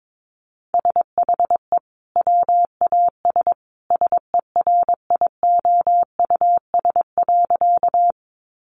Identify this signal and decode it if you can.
Morse code, heard as a keyed tone, about 22 wpm, 700 Hz. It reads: SHE WAS SERIOUS.
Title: Sentences from Top 1100 Words 22wpm